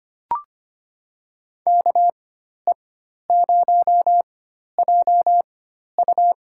Morse code, heard as a keyed tone, about 25 wpm, spaced as for 12 wpm, 700 Hz. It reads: KE0JU
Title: USA Call Signs - 4x Character Spacing - 25wpm